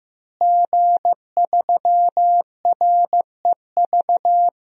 Morse code, heard as a keyed tone, about 15 wpm, 700 Hz. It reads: G3REV
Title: DXCC-100 Call Signs - 5 Characters - Repeated 3x - 15wpm